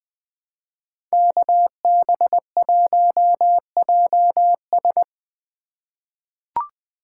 Morse code, heard as a keyed tone, about 20 wpm, 700 Hz. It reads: KB1JS
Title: USA Call Signs - 5 Characters - 2x Character Spacing - 20wpm